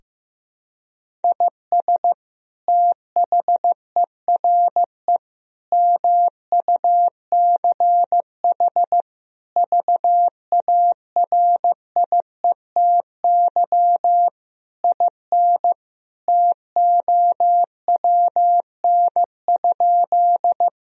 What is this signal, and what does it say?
Morse code, heard as a keyed tone, about 15 wpm, 700 Hz. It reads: IS THERE MUCH VARIETY IN TOWN?
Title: Sentences from Top 1000 Words - Review Part 4 - 15wpm